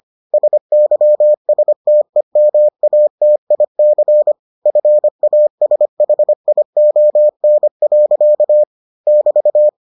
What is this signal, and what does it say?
Morse code, heard as a keyed tone, about 25 wpm, 600 Hz. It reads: SYSTEMATIC FASHION. <BT>